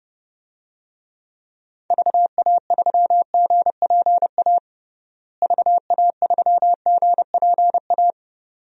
Morse code, heard as a keyed tone, about 32 wpm, 700 Hz. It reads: VA3GPA VA3GPA
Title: DXCC-100 Call Signs - 6 Characters - Repeated 2x - 30wpm